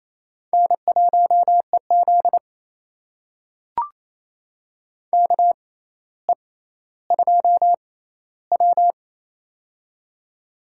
Morse code, heard as a keyed tone, about 28 wpm, spaced as for 14 wpm, 700 Hz. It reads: N1EZ KE2W
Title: USA Call Signs - 4 Characters - 6x Character Spacing - 28wpm